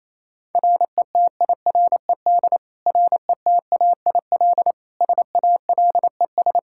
Morse code, heard as a keyed tone, about 27 wpm, 700 Hz. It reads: RETIRED RETAIL SALES
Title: Practice QSO - Exchange 3 - More Info - 28wpm